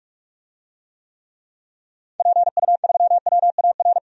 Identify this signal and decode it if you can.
Morse code, heard as a keyed tone, about 46 wpm, 700 Hz. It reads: PU3WAR
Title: DXCC-100 Call Signs - 6 Characters - 45wpm